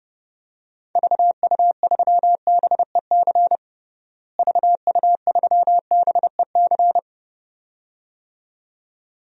Morse code, heard as a keyed tone, about 32 wpm, 700 Hz. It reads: VU3BEC VU3BEC
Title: DXCC-100 Call Signs - 6 Characters - Repeated 2x - 30wpm